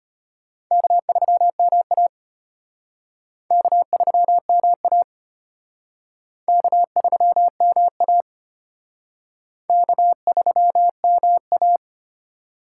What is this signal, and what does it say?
Morse code, heard as a keyed tone, about 33 wpm, 700 Hz. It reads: K3MA K3MA K3MA K3MA